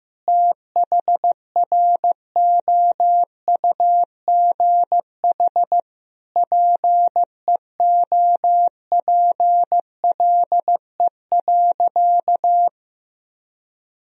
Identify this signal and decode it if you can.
Morse code, heard as a keyed tone, about 16 wpm, 700 Hz. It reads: THROUGH PEOPLE.
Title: Sentences from Top 200 Words and Abbreviations 15wpm